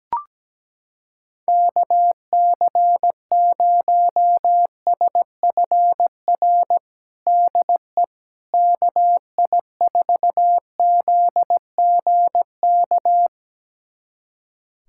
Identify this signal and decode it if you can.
Morse code, heard as a keyed tone, about 17 wpm, 700 Hz. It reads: KC0SFR DE KI4ZGK
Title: USA Callsign DE Callsign - 6 Characters - 17wpm